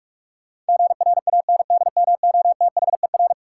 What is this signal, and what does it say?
Morse code, heard as a keyed tone, about 46 wpm, 700 Hz. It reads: GRANDMOTHER
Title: Sentences from Top 1200 Words - New Word Drill - 45wpm